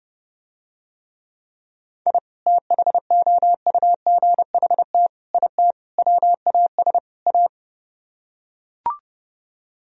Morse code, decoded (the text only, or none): I THOUGHT IT WAS A